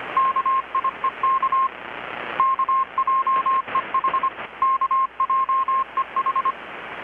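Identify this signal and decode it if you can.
Morse code, heard as a keyed tone, about 25 wpm, 1060 Hz, with radio noise. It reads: KIEK KJEH KJEH
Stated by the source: Morse from KPH's Night of Nights XX maritime radiotelegraphy event, received in Chiapas, Mexico about 3,800 km away on a Kenwood R-600 with a grounded 10 m long-wire antenna.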